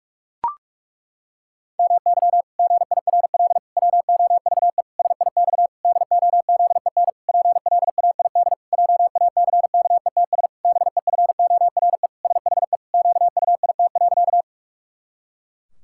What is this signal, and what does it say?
Morse code, heard as a keyed tone, about 41 wpm, 700 Hz. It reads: MY GIRL WOVE SIX DOZEN PLAID JACKETS BEFORE SHE QUIT.
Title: Pangram Challenge v2 - 45wpm